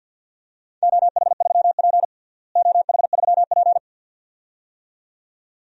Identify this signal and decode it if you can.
Morse code, heard as a keyed tone, about 52 wpm, 700 Hz. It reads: OH3P OH3P
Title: DXCC-100 Call Signs - 4 Characters - Repeated 2x - 50wpm